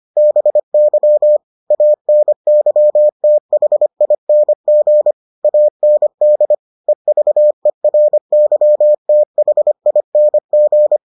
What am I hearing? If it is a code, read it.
Morse code, heard as a keyed tone, about 25 wpm, 600 Hz. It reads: BY ANYTHING AND EVERYTHING